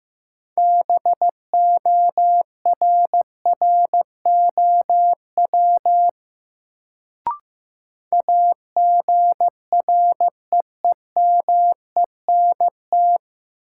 Morse code, heard as a keyed tone, about 15 wpm, 700 Hz. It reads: BORROW AGREEMENT
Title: Sentences from Top 1500 Words - New Word Drill - 15wpm